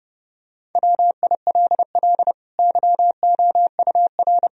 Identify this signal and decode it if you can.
Morse code, heard as a keyed tone, about 30 wpm, 700 Hz. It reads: WILL YOUR
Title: Sentences from Top 1000 Words - Review Part 3 - 30wpm